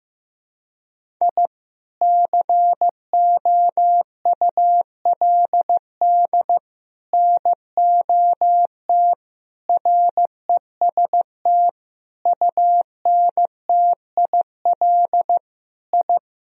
Morse code, heard as a keyed tone, about 15 wpm, 700 Hz. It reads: I COULD NOT REST UNTIL I
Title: Sentences from Top 1000 Words - Review Part 1 - 15wpm